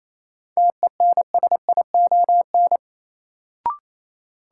Morse code, heard as a keyed tone, about 28 wpm, 700 Hz. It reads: TENSION
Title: Sentences from Top 1600 Words - New Word Drill Alt - 28wpm